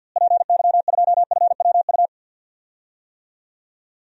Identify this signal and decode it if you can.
Morse code, heard as a keyed tone, about 53 wpm, 700 Hz. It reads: PY2FWU
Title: DXCC-100 Call Signs - 6 Characters - Repeated 2x - 50wpm